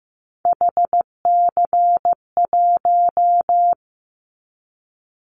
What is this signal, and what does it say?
Morse code, heard as a keyed tone, about 15 wpm, 700 Hz. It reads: HC1
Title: Callsign Prefixes for Top 100 DXCC - 15wpm